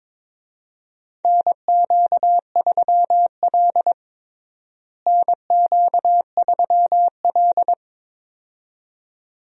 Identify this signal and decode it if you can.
Morse code, heard as a keyed tone, about 22 wpm, 700 Hz. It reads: NQ3L NQ3L
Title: USA Call Signs - 4 Characters - Repeated 2x - 22wpm